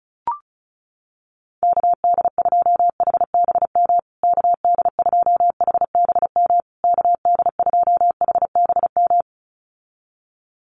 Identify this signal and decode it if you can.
Morse code, heard as a keyed tone, about 34 wpm, 700 Hz. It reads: KD2HBM KD2HBM KD2HBM
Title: USA Call Signs Repeated 3x 35wpm